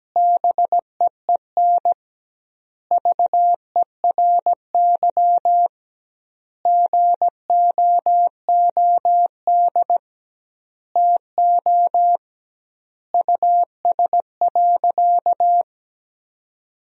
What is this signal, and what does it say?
Morse code, heard as a keyed tone, about 17 wpm, 700 Hz. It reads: BEEN VERY GOOD TO US.
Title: 2x Word-Spacing - Sentences from Top 200 Words - 17wpm